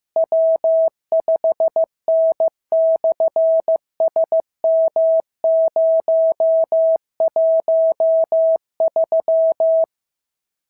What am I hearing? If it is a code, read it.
Morse code, heard as a keyed tone, about 15 wpm, 690 Hz. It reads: W5N/SM013
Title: SOTA QSO Practice 15wpm